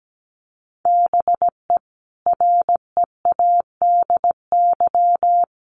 Morse code, heard as a keyed tone, about 17 wpm, 700 Hz. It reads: BE READY
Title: Sentences from Top 1100 Words 17wpm